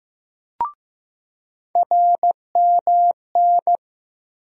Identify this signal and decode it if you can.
Morse code, heard as a keyed tone, about 15 wpm, 700 Hz. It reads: RMN
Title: USA Call Sign Postfixes - 3 Letters 15wpm